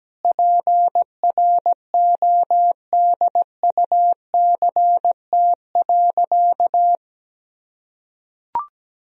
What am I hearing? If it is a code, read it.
Morse code, heard as a keyed tone, about 17 wpm, 700 Hz. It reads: PRODUCT.
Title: Sentences from Top 1000 Words - Review Part 2 - 17wpm